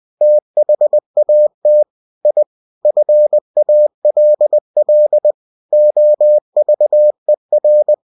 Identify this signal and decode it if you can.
Morse code, heard as a keyed tone, about 20 wpm, 600 Hz. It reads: THAT I FALL OVER
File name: cw-books / winnie-the-pooh / winnie-20wpm-page-0076